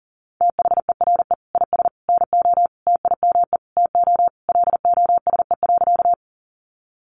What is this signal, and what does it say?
Morse code, heard as a keyed tone, about 40 wpm, 700 Hz. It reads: THERE IS NO TIME TO LOSE.
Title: Sentences from Top 1100 Words 40wpm